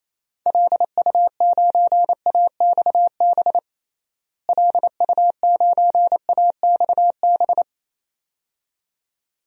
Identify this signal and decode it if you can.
Morse code, heard as a keyed tone, about 28 wpm, 700 Hz. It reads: LU9AXB LU9AXB
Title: DXCC-100 Call Signs - 6 Characters - Repeated 2x - 28wpm